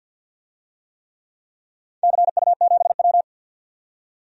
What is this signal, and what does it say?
Morse code, heard as a keyed tone, about 46 wpm, 700 Hz. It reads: CU7W
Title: DXCC-100 Call Signs - 4 Characters - 50wpm